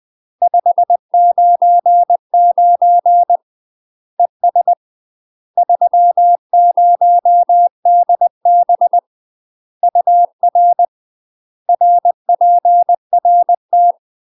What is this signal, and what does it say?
Morse code, heard as a keyed tone, about 20 wpm, 700 Hz. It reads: 599 ES 30DB UR RPRT
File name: UK MASTER QSOs TRAINING / 20WPM / UK MASTER QSOs 0028